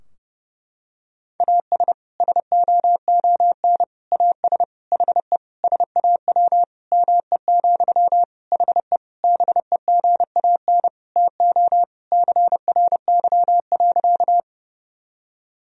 Morse code, heard as a keyed tone, about 30 wpm, 700 Hz. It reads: AS SOON AS HE SAW ME, HE BEGAN TO CRY.
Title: Sentences from Top 1000 Words - Review Part 2 - 30wpm